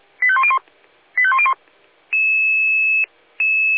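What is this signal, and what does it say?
Five-tone selective call (ZVEI), heard as quick runs of five tones, tones of about 70 ms. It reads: 85101 85101
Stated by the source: Five-tone ZVEI selective call 85201 received on an SDR.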